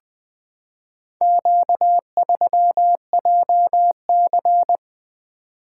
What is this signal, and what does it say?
Morse code, heard as a keyed tone, about 21 wpm, 700 Hz. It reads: Q3JC
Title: Four Letter-Number Combinations - 20wpm